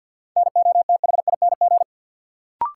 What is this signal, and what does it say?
Morse code, heard as a keyed tone, about 53 wpm, 700 Hz. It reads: NOTHING
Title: Sentences from Top 500 Words - New Word Drill - 50wpm